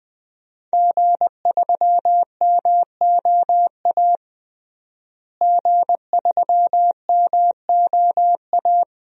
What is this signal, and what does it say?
Morse code, heard as a keyed tone, about 21 wpm, 700 Hz. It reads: G3MOA G3MOA
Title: DXCC-100 Call Signs - 5 Characters - Repeated 3x - 20wpm